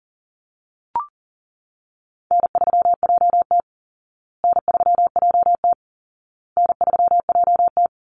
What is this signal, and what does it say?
Morse code, heard as a keyed tone, about 40 wpm, 700 Hz. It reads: N3JT N3JT N3JT
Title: Call Signs Repeated 2x - 40wpm